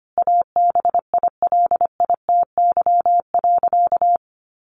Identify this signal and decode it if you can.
Morse code, heard as a keyed tone, about 25 wpm, 700 Hz. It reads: ABILITY.
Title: Sentences from Top 1200 Words 25wpm